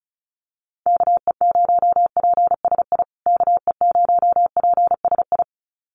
Morse code, heard as a keyed tone, about 34 wpm, 700 Hz. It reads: KE0PSI KE0PSI
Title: USA Call Signs Repeated 2x 35wpm